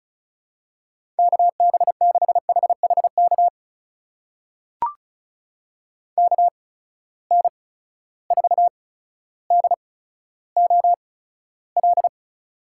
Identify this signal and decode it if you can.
Morse code, heard as a keyed tone, about 36 wpm, spaced as for 15 wpm, 700 Hz. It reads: KB6HHK KN4DOL
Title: USA Call Signs - 8x Character Spacing - 35wpm